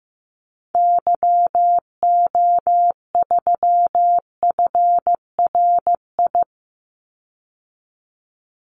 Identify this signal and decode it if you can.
Morse code, heard as a keyed tone, about 15 wpm, 700 Hz. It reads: YO3FRI
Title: Call Signs Repeated 2x - 15wpm